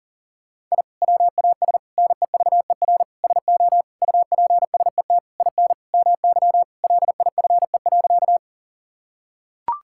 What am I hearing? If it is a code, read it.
Morse code, heard as a keyed tone, about 40 wpm, 700 Hz. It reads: I WAS NEVER SO UPSET IN MY LIFE.
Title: Sentences from Top 1000 Words - Review Part 5 - 40wpm